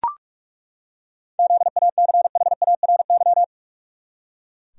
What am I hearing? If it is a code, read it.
Morse code, heard as a keyed tone, about 43 wpm, 700 Hz. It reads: ZACHARY
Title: USA Common Names - 45wpm